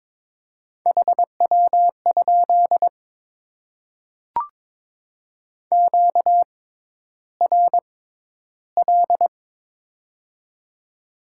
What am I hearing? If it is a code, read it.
Morse code, heard as a keyed tone, about 22 wpm, spaced as for 11 wpm, 700 Hz. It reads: HW? QRL
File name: Top 20 QSO Elements - 6x Character Spacing - 22wpm